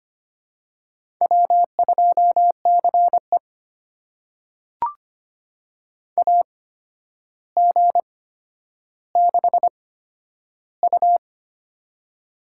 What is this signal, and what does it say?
Morse code, heard as a keyed tone, about 25 wpm, spaced as for 11 wpm, 700 Hz. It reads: W2CE AG6U
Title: USA Call Signs - 4 Characters - 8x Character Spacing - 25wpm